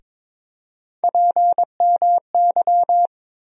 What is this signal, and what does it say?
Morse code, heard as a keyed tone, about 22 wpm, 700 Hz. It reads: PMY
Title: USA Call Sign Postfixes - 3 Letters 22wpm